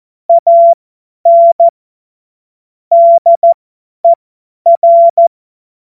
Morse code, heard as a keyed tone, about 14 wpm, spaced as for 10 wpm, 680 Hz. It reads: AN DER
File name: Morsekurs / Witze / Witz 28. März 2023 50 F70